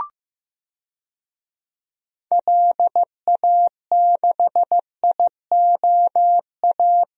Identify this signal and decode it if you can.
Morse code, heard as a keyed tone, about 15 wpm, 690 Hz. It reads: LA6IOA
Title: DXCC-100 Call Signs - 6 Characters - 15wpm